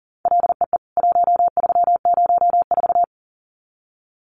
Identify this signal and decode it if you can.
Morse code, heard as a keyed tone, about 40 wpm, 700 Hz. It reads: LEE 1304
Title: CWT Exchange 40wpm